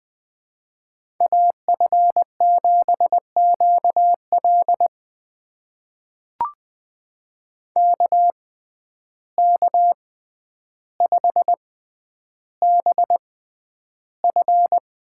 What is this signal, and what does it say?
Morse code, heard as a keyed tone, about 20 wpm, spaced as for 10 wpm, 700 Hz. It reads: AF7QL KK5BF
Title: USA Call Signs - 5 Characters - 6x Character Spacing - 20wpm